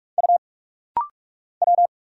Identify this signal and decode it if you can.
Morse code, heard as a keyed tone, about 44 wpm, 700 Hz. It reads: U W
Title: Single Letter-Number - Rapid-Fire 45wpm